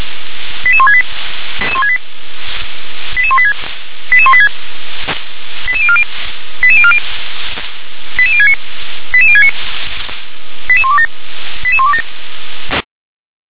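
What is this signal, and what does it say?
Five-tone selective call (ZVEI), heard as quick runs of five tones, tones of about 70 ms. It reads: 88168 88186 88186 88040 88040 88069 88069 88127 88127
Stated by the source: ZVEI five-tone call 88058 received on an SDR.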